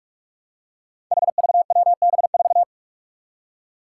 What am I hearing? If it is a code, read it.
Morse code, heard as a keyed tone, about 44 wpm, 700 Hz. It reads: HVWB4
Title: Five Challenging Letter-Number Combinations 45wpm